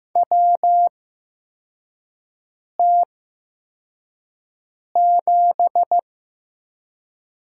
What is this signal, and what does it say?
Morse code, heard as a keyed tone, about 15 wpm, spaced as for 4 wpm, 700 Hz. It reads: WT7
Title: USA Call Signs - 4 Characters - 8x Character Spacing - 15wpm